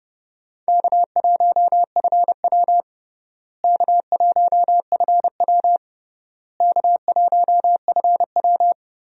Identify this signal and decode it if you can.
Morse code, heard as a keyed tone, about 30 wpm, 700 Hz. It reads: K1FW K1FW K1FW
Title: USA Call Signs - 4 Characters - Repeated 3x - 30wpm